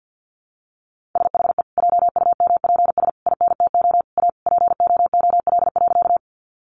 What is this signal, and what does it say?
Morse code, heard as a keyed tone, about 50 wpm, 700 Hz. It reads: SHE JUMPS INTO A POOL.
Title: Additional Easy Phrases 50wpm